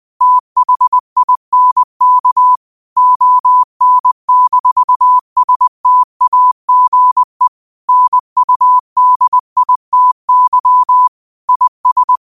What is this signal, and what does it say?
Morse code, heard as a keyed tone, about 20 wpm, 1000 Hz. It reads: THINK ON-STAGE NUDITY IS